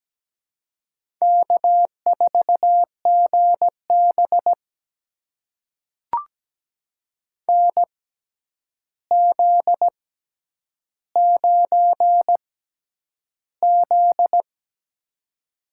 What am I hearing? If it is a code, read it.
Morse code, heard as a keyed tone, about 17 wpm, spaced as for 9 wpm, 700 Hz. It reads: K4GB NZ9Z
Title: USA Call Signs - 4 Characters - 6x Character Spacing - 17wpm